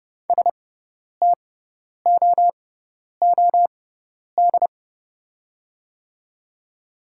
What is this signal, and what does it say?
Morse code, heard as a keyed tone, about 30 wpm, spaced as for 10 wpm, 700 Hz. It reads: STOOD
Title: 6x Character Spacing - Top 500 Words - 30wpm